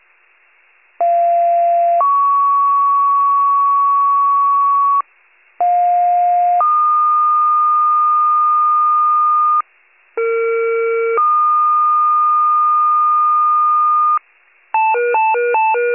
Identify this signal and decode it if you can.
Two-tone page, heard as a one-second tone, then a three-second tone, about 690 then 1120 Hz, about 710 then 1220 Hz, about 460 then 1220 Hz. It is heard from 1.0 to 14.2 s.
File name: wcalert